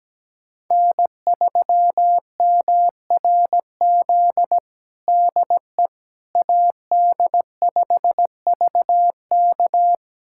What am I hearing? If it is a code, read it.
Morse code, heard as a keyed tone, about 17 wpm, 700 Hz. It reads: N3MRZ DE AD5VK
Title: USA Callsign DE Callsign - 5 Characters - 17wpm